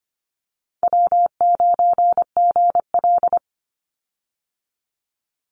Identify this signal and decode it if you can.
Morse code, heard as a keyed tone, about 25 wpm, 700 Hz. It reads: W9GL
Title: Easy Call Signs - 25wpm